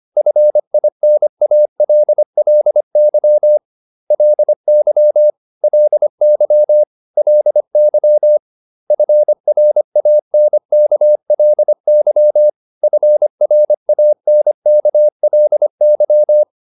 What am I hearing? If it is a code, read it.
Morse code, heard as a keyed tone, about 25 wpm, 600 Hz. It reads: FINALLY LY LY LY FRANKLY FRANKLY